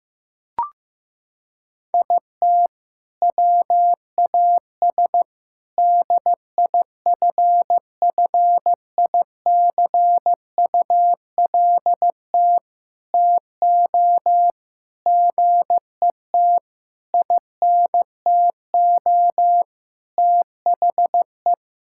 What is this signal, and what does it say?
Morse code, heard as a keyed tone, about 16 wpm, 700 Hz. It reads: IT WAS DIFFICULT TO GET INTO THE
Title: Sentences from Top 1600 Words 15wpm